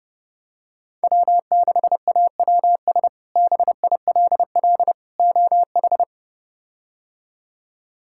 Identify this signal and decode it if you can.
Morse code, heard as a keyed tone, about 32 wpm, 700 Hz. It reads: W6AWS BILL OH
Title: K1USN SST QSO Practice - 30wpm